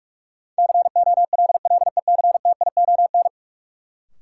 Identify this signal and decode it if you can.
Morse code, heard as a keyed tone, about 47 wpm, 700 Hz. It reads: COLLECTION
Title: Sentences from Top 1200 Words - New Word Drill Alt - 45wpm